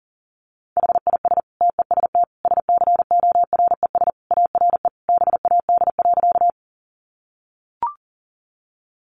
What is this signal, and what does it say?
Morse code, heard as a keyed tone, about 40 wpm, 700 Hz. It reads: HIS TEST SCORES ARE BAD.
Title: Additional Easy Phrases 40wpm